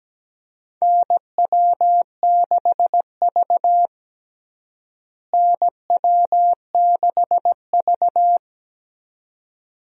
Morse code, heard as a keyed tone, about 17 wpm, 700 Hz. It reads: NW6V NW6V
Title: USA Call Signs - 4 Characters - Repeated 3x - 17wpm